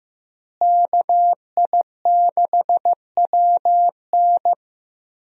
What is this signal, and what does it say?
Morse code, heard as a keyed tone, about 15 wpm, 700 Hz. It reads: KI6WN
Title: USA Call Signs - 5 Characters - Repeated 2x - 15wpm